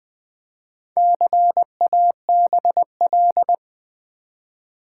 Morse code, heard as a keyed tone, about 20 wpm, 700 Hz. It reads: CABL
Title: Build A Word - 1 Letter At A Time - 5-Letter Words - 20wpm